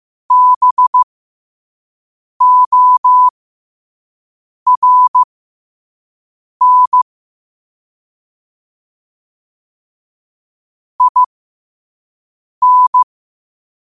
Morse code, heard as a keyed tone, about 15 wpm, spaced as for 5 wpm, 1000 Hz. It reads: BORN IN
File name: qotdfiles / 5wpm